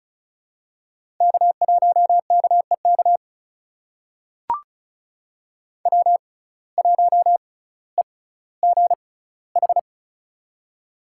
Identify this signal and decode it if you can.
Morse code, heard as a keyed tone, about 36 wpm, spaced as for 18 wpm, 700 Hz. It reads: K1KEK W1EGH
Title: USA Call Signs - 5 Characters - 6x Character Spacing - 35wpm